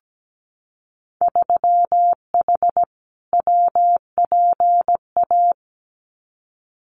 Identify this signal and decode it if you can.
Morse code, heard as a keyed tone, about 17 wpm, 700 Hz. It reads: 3H WPA
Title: Winter Field Day Exchanges Encore 17wpm